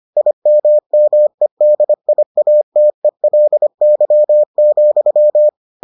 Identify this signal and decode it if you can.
Morse code, heard as a keyed tone, about 25 wpm, 600 Hz. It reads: IMMEDIATELY,